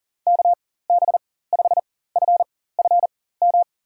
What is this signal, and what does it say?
Morse code, heard as a keyed tone, about 37 wpm, spaced as for 18 wpm, 700 Hz. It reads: KB5FFM
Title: USA Call Signs - 4x Character Spacing - 40wpm